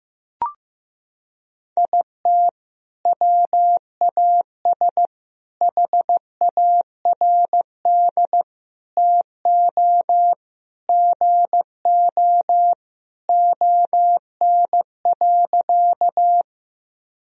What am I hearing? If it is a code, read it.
Morse code, heard as a keyed tone, about 15 wpm, 700 Hz. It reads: IT WAS HARD TO GO ON.
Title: Sentences from Top 1000 Words - Review Part 1 - 15wpm